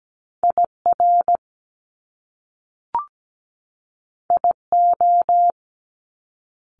Morse code, heard as a keyed tone, about 17 wpm, 700 Hz. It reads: IR IO
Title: Top Two Letter Combinations in Words 17wpm